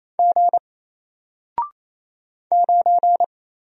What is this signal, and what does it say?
Morse code, heard as a keyed tone, about 28 wpm, 700 Hz. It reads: G 9
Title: Single Letter-Number Alt 28wpm